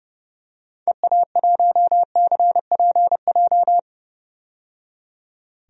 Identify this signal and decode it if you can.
Morse code, heard as a keyed tone, about 30 wpm, 700 Hz. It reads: EA1CPJ
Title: DXCC-100 Call Signs - 6 Characters - 30wpm